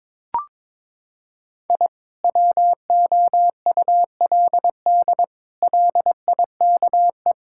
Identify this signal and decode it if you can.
Morse code, heard as a keyed tone, about 22 wpm, 700 Hz. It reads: I WOULD LIKE
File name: practice / 22wpm / Sets of 8 Words - 22wpm